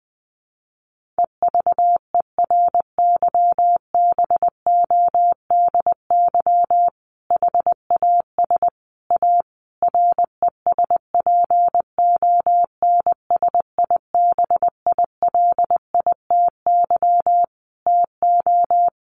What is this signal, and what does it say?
Morse code, heard as a keyed tone, about 19 wpm, 700 Hz. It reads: EVERYBODY HAS A RESPONSIBILITY TO
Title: Sentences from Top 1400 Words 20wpm